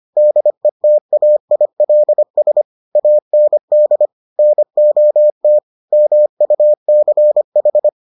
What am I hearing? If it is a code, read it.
Morse code, heard as a keyed tone, about 25 wpm, 600 Hz. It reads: DETAILS AND NOT MUCH